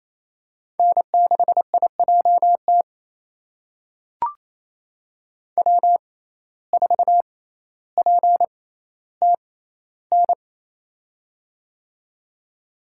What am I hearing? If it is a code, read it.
Morse code, heard as a keyed tone, about 28 wpm, spaced as for 14 wpm, 700 Hz. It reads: N6IJT W4PTN
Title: USA Call Signs - 5 Characters - 6x Character Spacing - 28wpm